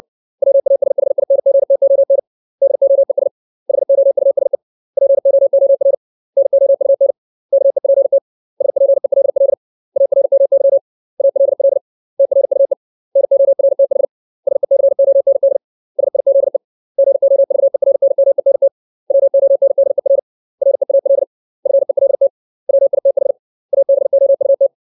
Morse code, heard as a keyed tone, about 61 wpm, 550 Hz. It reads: WASHINGTON DOES HOUSE POOR NOUN KEPT SPELL ARMY ALL TRUE NORTH SECOND SIZE GOVERNMENT WONDER REAL FELT WITH ABOUT
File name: 632 words / 632 Words 60WPM